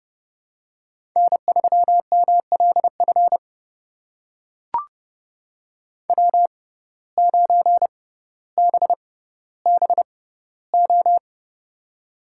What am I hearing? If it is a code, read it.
Morse code, heard as a keyed tone, about 30 wpm, spaced as for 15 wpm, 700 Hz. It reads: N3MLF W9BBO
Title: USA Call Signs - 5 Characters - 6x Character Spacing - 30wpm